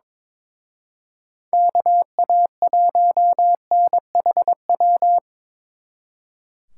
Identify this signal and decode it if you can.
Morse code, heard as a keyed tone, about 25 wpm, 700 Hz. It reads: KA1NHW
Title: Speed Racing - USA 2x3 Call Signs 22wpm